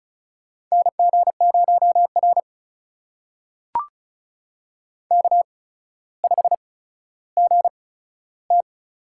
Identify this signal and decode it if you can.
Morse code, heard as a keyed tone, about 36 wpm, spaced as for 15 wpm, 700 Hz. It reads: NG0R K5GT
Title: USA Call Signs - 4 Characters - 8x Character Spacing - 35wpm